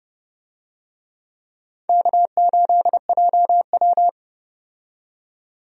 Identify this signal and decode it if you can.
Morse code, heard as a keyed tone, about 30 wpm, 700 Hz. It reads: K8JW
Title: USA Call Signs - 4 Characters - 30wpm